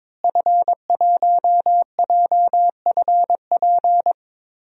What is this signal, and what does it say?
Morse code, heard as a keyed tone, about 22 wpm, 700 Hz. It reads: F1JFP
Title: DXCC-100 Call Signs - 5 Characters - Repeated 2x - 22wpm